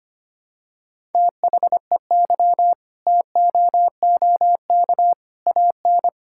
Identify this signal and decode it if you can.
Morse code, heard as a keyed tone, about 25 wpm, 700 Hz. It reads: THEY TOOK AN
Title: Sentences from Top 1000 Words - Review Part 6 - 25wpm